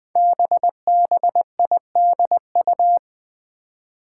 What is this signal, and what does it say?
Morse code, heard as a keyed tone, about 20 wpm, 700 Hz. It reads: BBIDU
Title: Five Challenging Letter-Number Combinations 20wpm